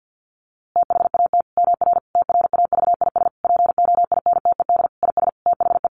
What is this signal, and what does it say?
Morse code, heard as a keyed tone, about 50 wpm, 700 Hz. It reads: THAT MR TRAVIS POINTED IS THE
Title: A Sound Of Thunder 50wpm